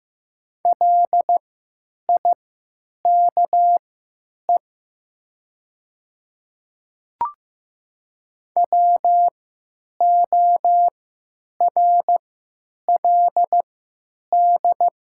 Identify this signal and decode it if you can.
Morse code, heard as a keyed tone, about 15 wpm, spaced as for 9 wpm, 700 Hz. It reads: LIKE WORLD
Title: 3x Character Spacing - Top 300 Words - 15wpm